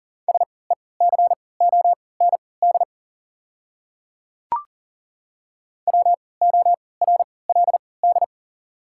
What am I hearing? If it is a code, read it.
Morse code, heard as a keyed tone, about 40 wpm, spaced as for 23 wpm, 690 Hz. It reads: SECOND WORLD
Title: 3x Character Spacing - Top 300 Words - 40wpm